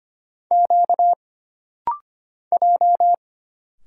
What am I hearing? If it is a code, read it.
Morse code, heard as a keyed tone, about 25 wpm, 700 Hz. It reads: Q J
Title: Single Letters - Rapid-Fire 25wpm